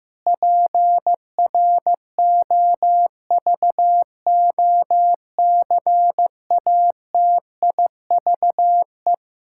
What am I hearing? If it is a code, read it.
Morse code, heard as a keyed tone, about 15 wpm, 700 Hz. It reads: PROVOCATIVE
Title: Pangram Challenge v2 - 15wpm